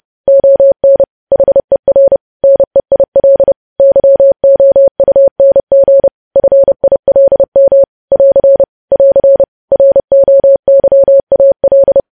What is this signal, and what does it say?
Morse code, heard as a keyed tone, about 30 wpm, 550 Hz. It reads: ON HER NEIL YOUNG FILM <AR> <AR> ROYAL
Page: news headlines in morse code at 30 wpm